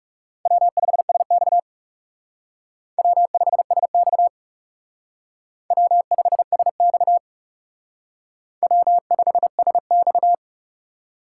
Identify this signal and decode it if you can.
Morse code, heard as a keyed tone, about 38 wpm, 700 Hz. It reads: W5SX W5SX W5SX W5SX